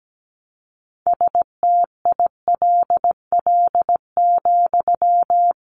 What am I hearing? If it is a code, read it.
Morse code, heard as a keyed tone, about 17 wpm, 700 Hz. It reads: STILL,
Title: Sentences from Top 1300 Words 17wpm